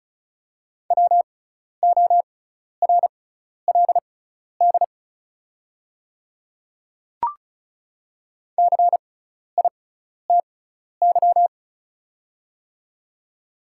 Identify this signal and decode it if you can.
Morse code, heard as a keyed tone, about 36 wpm, spaced as for 12 wpm, 700 Hz. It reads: WORLD CITY
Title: 6x Character Spacing - Top 300 Words - 35wpm